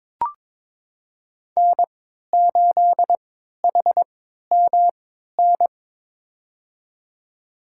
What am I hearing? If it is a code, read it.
Morse code, heard as a keyed tone, about 22 wpm, spaced as for 12 wpm, 700 Hz. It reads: N8HMN
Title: USA Call Signs - 5 Characters - 3x Character Spacing - 22wpm